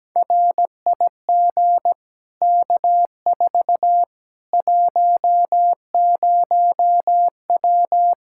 Morse code, heard as a keyed tone, about 17 wpm, 700 Hz. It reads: RIG K4 10W
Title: Practice QSO - Exchange 2 - Info - 17wpm